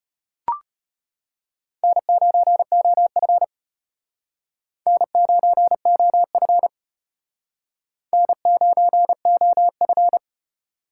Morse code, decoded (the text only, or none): N9OF N9OF N9OF